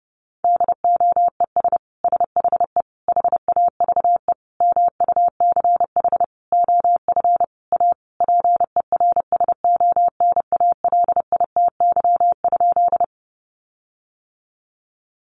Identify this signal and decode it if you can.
Morse code, heard as a keyed tone, about 30 wpm, 700 Hz. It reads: DOES SHE HAVE MUCH OF A PERSONALITY?
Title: Sentences from Top 1400 Words 30wpm